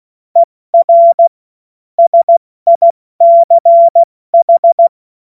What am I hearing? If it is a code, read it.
Morse code, heard as a keyed tone, about 16 wpm, 680 Hz. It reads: ER SICH
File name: Morsekurs / Witze / Witz 06. Dezember 2022 70 F80